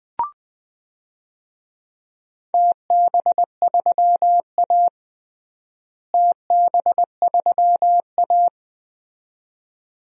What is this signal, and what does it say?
Morse code, heard as a keyed tone, about 20 wpm, 700 Hz. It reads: TB3A TB3A
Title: DXCC-100 Call Signs - 4 Characters - Repeated 2x - 20wpm